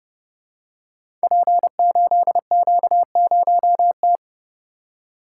Morse code, heard as a keyed tone, about 30 wpm, 700 Hz. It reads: P8Q0T
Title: Code Groups - 5 Letter:Numbers - 30wpm